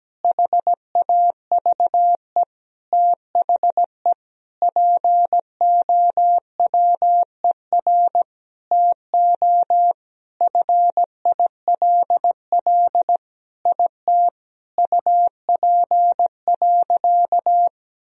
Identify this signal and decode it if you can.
Morse code, heard as a keyed tone, about 17 wpm, 700 Hz. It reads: HAVE THE POWER TO FILL IT UP.
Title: Sentences from Top 1000 Words - Review Part 3 - 17wpm